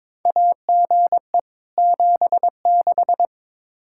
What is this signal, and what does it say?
Morse code, heard as a keyed tone, about 22 wpm, 700 Hz. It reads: AGE 76
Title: Practice QSO - Exchange 3 - More Info - 22wpm